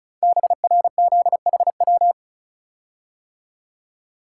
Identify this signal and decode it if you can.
Morse code, heard as a keyed tone, about 36 wpm, 700 Hz. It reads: BRZHW
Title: Code Groups - 5 Letters - 35wpm